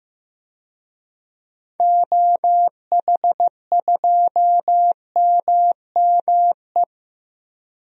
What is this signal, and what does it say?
Morse code, heard as a keyed tone, about 15 wpm, 700 Hz. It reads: OH2MME
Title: DXCC-100 Call Signs - 6 Characters - Repeated 3x - 15wpm